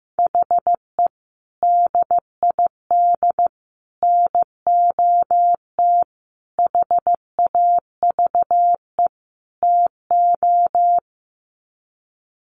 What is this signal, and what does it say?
Morse code, heard as a keyed tone, about 14 wpm, 700 Hz. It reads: HE DID NOT HAVE TO
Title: Sets of 5 Words 15wpm